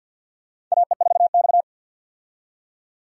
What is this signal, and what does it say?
Morse code, heard as a keyed tone, about 47 wpm, 700 Hz. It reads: AE4X
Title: USA Call Signs - 4 Characters - 50wpm